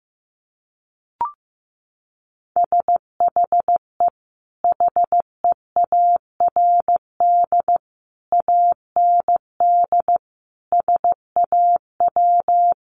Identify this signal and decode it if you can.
Morse code, heard as a keyed tone, about 15 wpm, 700 Hz. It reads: SHE HEARD AND SAW
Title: Sentences from Top 500 Words 15wpm